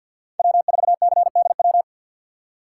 Morse code, heard as a keyed tone, about 53 wpm, 700 Hz. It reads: W4CDW
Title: USA Call Signs - 5 Characters - 2x Character Spacing - 50wpm